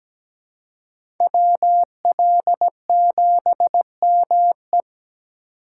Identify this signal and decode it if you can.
Morse code, heard as a keyed tone, about 20 wpm, 700 Hz. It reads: WL7ME
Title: Speed Racing - USA 2x2 Call Signs 17wpm